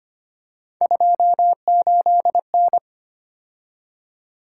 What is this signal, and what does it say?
Morse code, heard as a keyed tone, about 30 wpm, 700 Hz. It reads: 28N
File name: Speed-Racing - Three Letter-Number Combinations - 25wpm